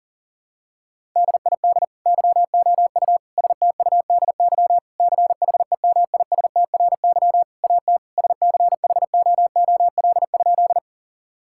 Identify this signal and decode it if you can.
Morse code, heard as a keyed tone, about 37 wpm, 700 Hz. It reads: DID YOU STUDY CHEMISTRY AT SCHOOL?
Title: Sentences from Top 1000 Words - Review Part 6 - 40wpm